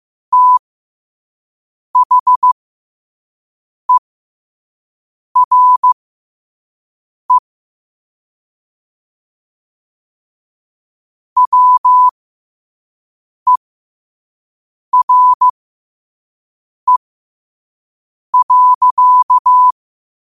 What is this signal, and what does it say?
Morse code, heard as a keyed tone, about 15 wpm, spaced as for 5 wpm, 1000 Hz. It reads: THERE WERE.
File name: qotdfiles / 5wpm